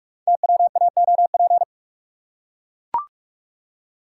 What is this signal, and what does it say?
Morse code, heard as a keyed tone, about 44 wpm, 700 Hz. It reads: TWAOP
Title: Code Groups - 5 Letters - 45wpm